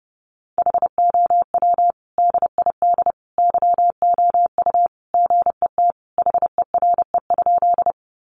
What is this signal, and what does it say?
Morse code, heard as a keyed tone, about 30 wpm, 700 Hz. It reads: HOW DID YOU GET HERE?
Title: Sentences from Top 200 Words 30wpm